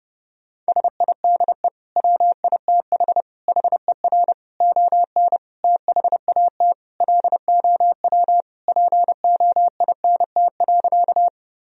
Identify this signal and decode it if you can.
Morse code, heard as a keyed tone, about 30 wpm, 700 Hz. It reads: SIDE WITH HER ON THAT LOW POINT.
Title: Sentences from Top 1000 Words - Review Part 1 - 30wpm